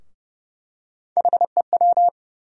Morse code, heard as a keyed tone, about 30 wpm, 700 Hz. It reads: HEW
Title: USA Call Sign Postfixes - 3 Letters 30wpm